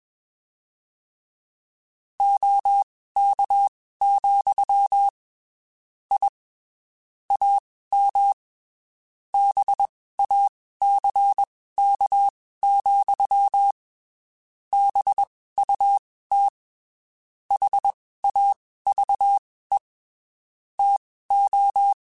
Morse code, heard as a keyed tone, about 21 wpm, spaced as for 15 wpm, 800 Hz. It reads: OK, I AM BACK, BUT HAVE TO
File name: (spaced) / QSOs Spaced-mp3s / kc7sac @20 sp